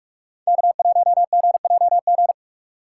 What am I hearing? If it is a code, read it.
Morse code, heard as a keyed tone, about 45 wpm, 700 Hz. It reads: K1GJG
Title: USA Call Signs - 6x Character Spacing - 45wpm